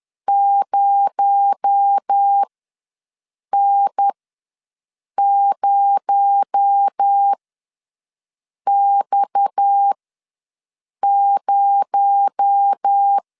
Morse code, heard as a keyed tone, about 10 wpm, spaced as for 6 wpm, 800 Hz. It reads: 0N0X0
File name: LEZIONI / 6 PKX0 / 6 PKX0 ws 10wpm 4220